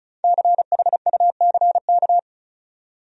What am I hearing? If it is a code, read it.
Morse code, heard as a keyed tone, about 34 wpm, 700 Hz. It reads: CHUCK
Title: USA Common Names - 35wpm